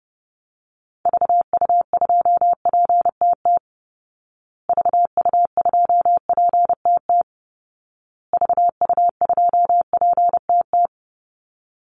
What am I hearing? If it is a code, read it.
Morse code, heard as a keyed tone, about 30 wpm, 700 Hz. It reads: VU2PTT VU2PTT VU2PTT
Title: Call Signs Repeated 2x - 30wpm